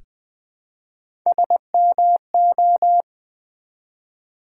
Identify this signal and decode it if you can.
Morse code, heard as a keyed tone, about 20 wpm, 700 Hz. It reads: SMO
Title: Build A Word - 1 Letter At A Time - 5-Letter Words - 20wpm